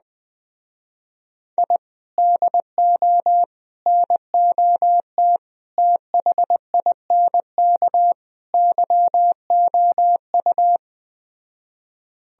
Morse code, heard as a keyed tone, about 20 wpm, 700 Hz. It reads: I DO NOT THINK YOU
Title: Speed-Racing - Sets of 5 Words 17wpm